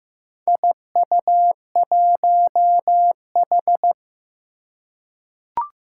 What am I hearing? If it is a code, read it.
Morse code, heard as a keyed tone, about 15 wpm, 700 Hz. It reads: IU1H
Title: Four Challenging Letter-Number Combinations 15wpm